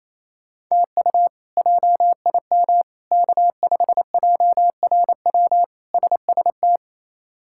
Morse code, heard as a keyed tone, about 28 wpm, 700 Hz. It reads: TU JIM K5JRW SST
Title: K1USN SST QSO Practice - 28wpm